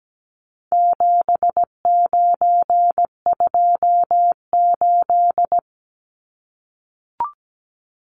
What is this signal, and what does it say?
Morse code, heard as a keyed tone, about 17 wpm, 700 Hz. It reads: 7928
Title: Numbers - 4-Digit 17wpm